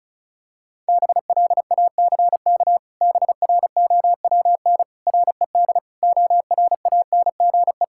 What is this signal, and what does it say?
Morse code, heard as a keyed tone, about 34 wpm, 700 Hz. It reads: BLACK BROWN RED ORANGE
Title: Pangram Challenge v2 - 35wpm